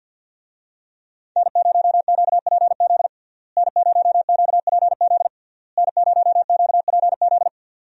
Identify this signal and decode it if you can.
Morse code, heard as a keyed tone, about 52 wpm, 700 Hz. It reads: N0QPZ N0QPZ N0QPZ
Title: USA Call Signs - 5 Characters - Repeated 3x - 50wpm